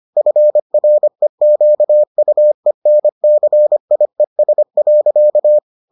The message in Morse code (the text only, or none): FREQUENCIES.